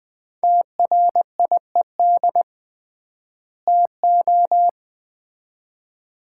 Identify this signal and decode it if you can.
Morse code, heard as a keyed tone, about 20 wpm, 700 Hz. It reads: TRIED TO
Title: 3x Word-Spacing - Sets of 2 Words - 20wpm